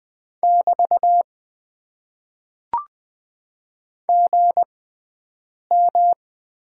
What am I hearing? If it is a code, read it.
Morse code, heard as a keyed tone, about 20 wpm, spaced as for 7 wpm, 700 Hz. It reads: <BT> GM
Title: Top 90 QSO Elements - 6x Character Spacing - 20wpm